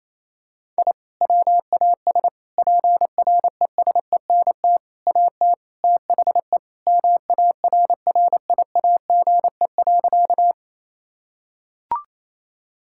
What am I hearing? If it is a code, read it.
Morse code, heard as a keyed tone, about 28 wpm, 700 Hz. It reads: I WAS PRESENT AT THE MARRIAGE.
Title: Sentences from Top 1000 Words - Review Part 5 - 28wpm